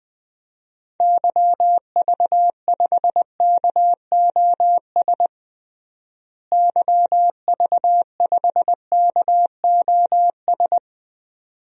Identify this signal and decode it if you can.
Morse code, heard as a keyed tone, about 20 wpm, 700 Hz. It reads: YV5KOS YV5KOS
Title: DXCC-100 Call Signs - 6 Characters - Repeated 2x - 20wpm